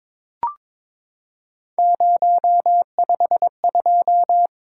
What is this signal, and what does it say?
Morse code, heard as a keyed tone, about 23 wpm, 700 Hz. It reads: 052
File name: practice / 22wpm / Three Letter-Number Combinations - 22wpm